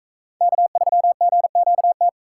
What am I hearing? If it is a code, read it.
Morse code, heard as a keyed tone, about 44 wpm, 700 Hz. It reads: K3GQT